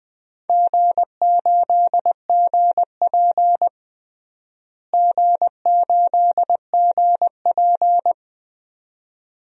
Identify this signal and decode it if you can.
Morse code, heard as a keyed tone, about 20 wpm, 700 Hz. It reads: G8GP G8GP
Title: DXCC-100 Call Signs - 4 Characters - Repeated 2x - 20wpm